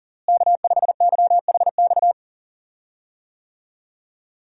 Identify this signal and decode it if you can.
Morse code, heard as a keyed tone, about 40 wpm, 700 Hz. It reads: K5YHX
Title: USA Call Signs - 5 Characters - Repeated 3x - 40wpm